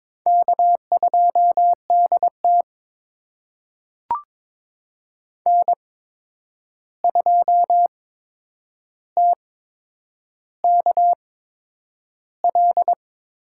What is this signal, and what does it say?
Morse code, heard as a keyed tone, about 22 wpm, spaced as for 9 wpm, 700 Hz. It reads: K2DT N2TKL
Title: USA Call Signs - 8x Character Spacing - 22wpm